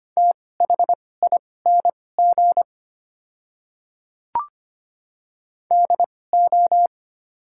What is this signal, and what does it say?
Morse code, heard as a keyed tone, about 25 wpm, spaced as for 18 wpm, 700 Hz. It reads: THING DO
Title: 2x Character Spacing - Top 100 Words - 25wpm